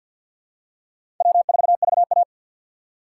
Morse code, heard as a keyed tone, about 52 wpm, 700 Hz. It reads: W4VA
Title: USA Call Signs - 4 Characters - 2x Character Spacing - 50wpm